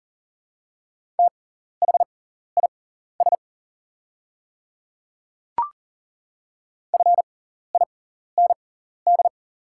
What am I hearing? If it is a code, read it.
Morse code, heard as a keyed tone, about 40 wpm, spaced as for 14 wpm, 700 Hz. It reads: THIS FIND
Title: 6x Character Spacing - Top 100 Words - 40wpm